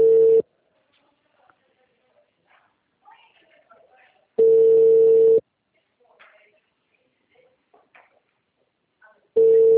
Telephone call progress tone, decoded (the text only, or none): ringback tone